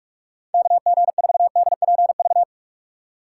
Morse code, heard as a keyed tone, about 46 wpm, 700 Hz. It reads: KG4DPV
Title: USA Call Signs - 6 Characters - 3x Character Spacing - 45wpm